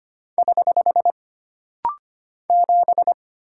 Morse code, heard as a keyed tone, about 25 wpm, 700 Hz. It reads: <HH> 7
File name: Single Character (Letter:Number:Punctuation:Prosign) - Rapid-Fire - 25wpm